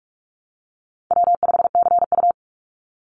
Morse code, heard as a keyed tone, about 44 wpm, 700 Hz. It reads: W5CU
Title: Call Signs Repeated 3x - 45wpm